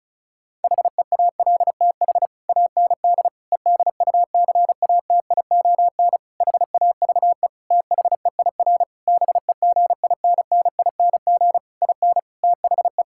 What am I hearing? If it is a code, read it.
Morse code, heard as a keyed tone, about 36 wpm, 700 Hz. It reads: HEALTH AND EDUCATION HAVE THEIR BEGINNING IN THE
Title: Sentences from Top 1000 Words - Review Part 6 - 35wpm